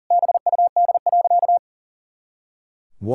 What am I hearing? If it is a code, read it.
Morse code, heard as a keyed tone, about 37 wpm, 700 Hz. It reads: BUD.
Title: Pangram Challenge v2 - 40wpm